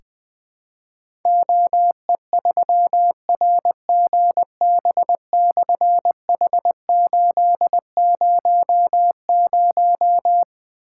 Morse code, heard as a keyed tone, about 20 wpm, 700 Hz. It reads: OE3RGB/H800
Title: Difficult Call Signs 20wpm